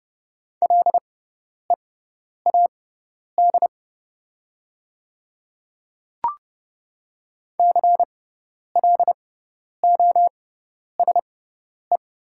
Morse code, heard as a keyed tone, about 30 wpm, spaced as for 10 wpm, 690 Hz. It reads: LEAD CLOSE
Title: 6x Character Spacing - Top 500 Words - 30wpm